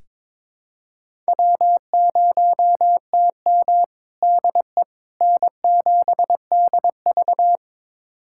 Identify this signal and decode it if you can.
Morse code, heard as a keyed tone, about 22 wpm, 700 Hz. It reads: W0TM DE N7DV
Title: USA Callsign DE Callsign - 4 Characters - 22wpm